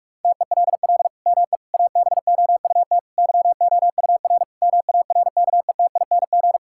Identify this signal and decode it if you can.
Morse code, heard as a keyed tone, about 46 wpm, 700 Hz. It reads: TELL ME ABOUT YOUR MARKETING
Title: Sentences from Top 1000 Words - Review Part 5 - 45wpm